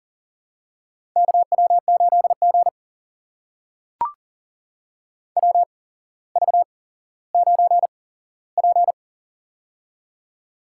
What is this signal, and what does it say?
Morse code, heard as a keyed tone, about 40 wpm, spaced as for 17 wpm, 700 Hz. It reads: KW8G WV9P
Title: USA Call Signs - 4 Characters - 8x Character Spacing - 40wpm